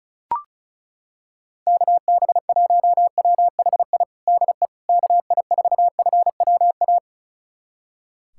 Morse code, heard as a keyed tone, about 34 wpm, 700 Hz. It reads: KB1WHI DE KI4FWA
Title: USA Callsign DE Callsign - 6 Characters - 35wpm